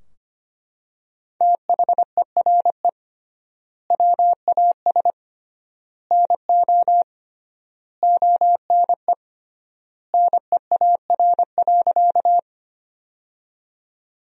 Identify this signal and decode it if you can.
Morse code, heard as a keyed tone, about 25 wpm, 700 Hz. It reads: THERE WAS NO ONE NEAR.
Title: 3x Word-Spacing - Sentences from Top 200 Words - 25wpm